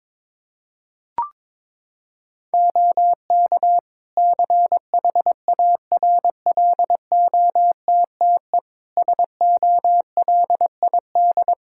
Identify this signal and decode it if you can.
Morse code, heard as a keyed tone, about 22 wpm, 700 Hz. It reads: OK CHARLOTTE SOLID
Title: Practice QSO - Exchange 3 - More Info - 22wpm